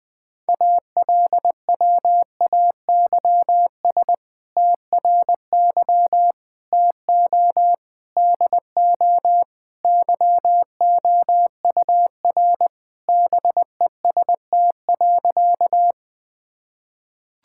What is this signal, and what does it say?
Morse code, heard as a keyed tone, about 20 wpm, 700 Hz. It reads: ALWAYS TRY TO DO YOUR BEST.
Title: Sentences with 2-Word Phrases - Spoken Beginning - 20wpm